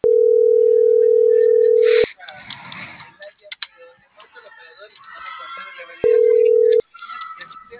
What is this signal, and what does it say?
Ringback tone (North America), 440 + 480 Hz.